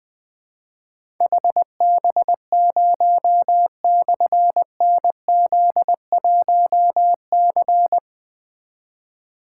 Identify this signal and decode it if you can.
Morse code, heard as a keyed tone, about 20 wpm, 700 Hz. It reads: HB0/NZ1C
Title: Difficult Call Signs 20wpm